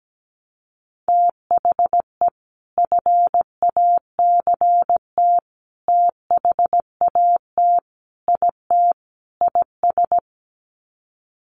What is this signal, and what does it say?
Morse code, heard as a keyed tone, about 17 wpm, 700 Hz. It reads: THE FACT THAT IT IS
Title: Sets of 5 Words Encore2 17wpm